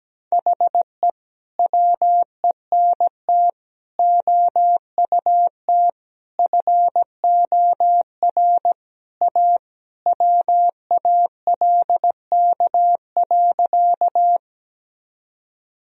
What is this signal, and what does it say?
Morse code, heard as a keyed tone, about 17 wpm, 700 Hz. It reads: HE WENT OUT FOR A WALK.
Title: Sentences from Top 1000 Words - Review Part 1 - 17wpm